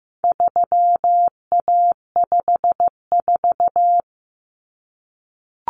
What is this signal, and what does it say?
Morse code, heard as a keyed tone, about 15 wpm, 700 Hz. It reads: 3A54
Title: Common Cut Numbers - 4-Digit 15wpm